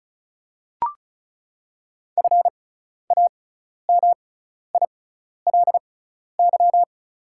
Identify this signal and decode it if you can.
Morse code, heard as a keyed tone, about 34 wpm, spaced as for 12 wpm, 700 Hz. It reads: FAMILY
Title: 6x Character Spacing - Top 500 Words - 35wpm